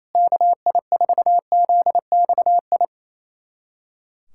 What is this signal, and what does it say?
Morse code, heard as a keyed tone, about 28 wpm, 700 Hz. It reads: KI4ZXI
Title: USA Callsign DE Callsign - 6 Characters - 28wpm